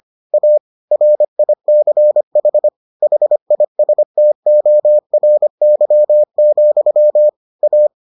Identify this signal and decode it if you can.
Morse code, heard as a keyed tone, about 25 wpm, 600 Hz. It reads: A RICH HISTORY, A